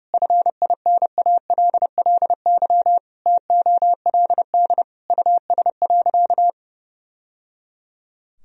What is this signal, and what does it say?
Morse code, heard as a keyed tone, about 32 wpm, 700 Hz. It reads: FINALLY TOLD US.
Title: Sentences from Top 1000 Words - Review Part 3 - 30wpm